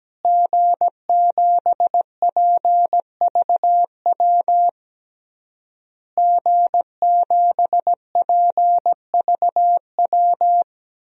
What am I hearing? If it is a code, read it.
Morse code, heard as a keyed tone, about 17 wpm, 700 Hz. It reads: G7PVW G7PVW
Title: DXCC-100 Call Signs - 5 Characters - Repeated 3x - 17wpm